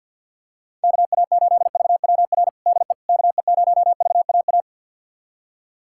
Morse code, heard as a keyed tone, about 52 wpm, 700 Hz. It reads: KA8VWR DE KE0VAA
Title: USA Callsign DE Callsign - 6 Characters - 50wpm